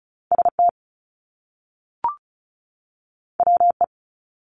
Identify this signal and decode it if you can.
Morse code, heard as a keyed tone, about 36 wpm, 700 Hz. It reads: ST WE
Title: Top Two Letter Combinations in Words 35wpm